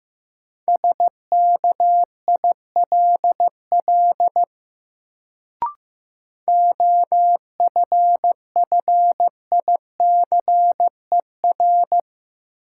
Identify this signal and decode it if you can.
Morse code, heard as a keyed tone, about 15 wpm, 700 Hz. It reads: SKILL OFFICER
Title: Sentences from Top 1200 Words - New Word Drill Alt - 15wpm